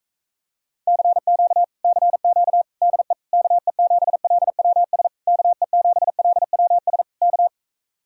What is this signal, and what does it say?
Morse code, heard as a keyed tone, about 46 wpm, 700 Hz. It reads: CQ CQ DE KE7LWS KE7LWS K